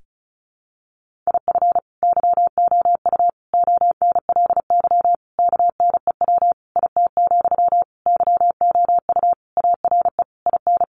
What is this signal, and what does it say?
Morse code, heard as a keyed tone, about 34 wpm, 700 Hz. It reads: IF YOU ONLY KNEW IT, YOU ARE IN
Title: Sentences from Top 1500 Words 35wpm